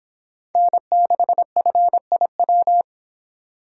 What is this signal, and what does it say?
Morse code, heard as a keyed tone, about 25 wpm, 700 Hz. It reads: N6FIW